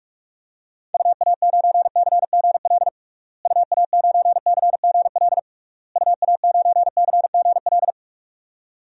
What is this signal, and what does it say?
Morse code, heard as a keyed tone, about 46 wpm, 700 Hz. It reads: UA9CGL UA9CGL UA9CGL
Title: DXCC-100 Call Signs - 6 Characters - Repeated 3x - 45wpm